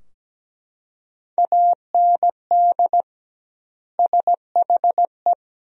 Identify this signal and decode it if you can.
Morse code, heard as a keyed tone, about 17 wpm, 700 Hz. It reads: AND SHE
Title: 2x Word-Spacing - Sentences from Top 300 Words - 17wpm